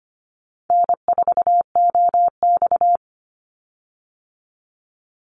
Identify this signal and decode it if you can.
Morse code, heard as a keyed tone, about 25 wpm, 700 Hz. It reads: N4OX
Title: Easy Call Signs - 25wpm